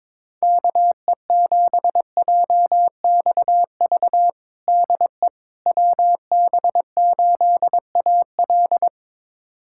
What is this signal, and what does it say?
Morse code, heard as a keyed tone, about 21 wpm, 700 Hz. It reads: KE7JXV DE WB8AL
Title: USA Callsign DE Callsign 22wpm